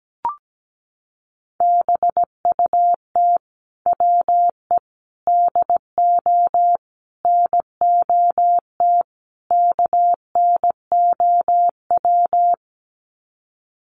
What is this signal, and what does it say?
Morse code, heard as a keyed tone, about 17 wpm, 700 Hz. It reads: BUT WE DO NOT KNOW
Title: Sets of 5 Words Encore2 17wpm